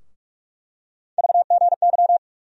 Morse code, heard as a keyed tone, about 46 wpm, 700 Hz. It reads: VGY
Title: USA Call Sign Postfixes - 3 Letters 45wpm